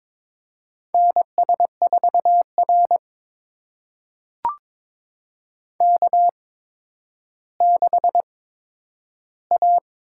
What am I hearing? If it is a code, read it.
Morse code, heard as a keyed tone, about 22 wpm, spaced as for 11 wpm, 700 Hz. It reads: NS4R K6A
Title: USA Call Signs - 4 Characters - 8x Character Spacing - 22wpm